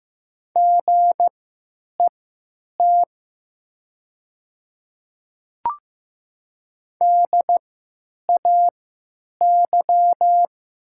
Morse code, heard as a keyed tone, about 15 wpm, spaced as for 9 wpm, 700 Hz. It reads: GET DAY
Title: 3x Character Spacing - Top 200 Words - 15wpm